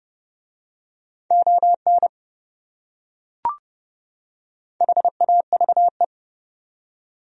Morse code, heard as a keyed tone, about 30 wpm, 700 Hz. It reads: ON HAVE